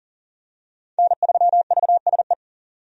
Morse code, heard as a keyed tone, about 40 wpm, 700 Hz. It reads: N3VSE
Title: USA Call Signs - 5 Characters - 40wpm